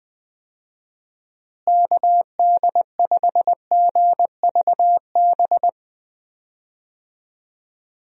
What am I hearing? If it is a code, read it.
Morse code, heard as a keyed tone, about 20 wpm, 700 Hz. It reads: KD5GVB
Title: USA Call Signs - 6 Characters - Repeated 2x - 20wpm